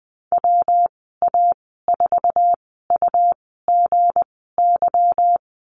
Morse code, heard as a keyed tone, about 20 wpm, spaced as for 15 wpm, 700 Hz. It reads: WA4UGY
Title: 161 Lesson 26g - Letter g - Call Signs 20wpm